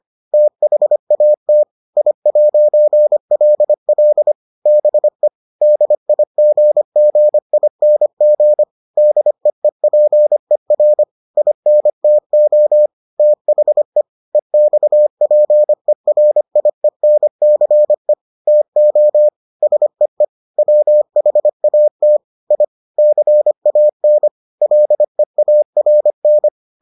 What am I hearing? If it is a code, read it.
Morse code, heard as a keyed tone, about 25 wpm, 600 Hz. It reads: THAT I'LL BE DIGGING DEEPER INTO THE EXPERIENCE TO SEE WHAT I CAN LEARN